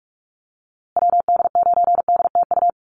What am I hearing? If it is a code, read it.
Morse code, heard as a keyed tone, about 44 wpm, 700 Hz. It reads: WD9DTU
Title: USA Call Signs Repeated 3x 45wpm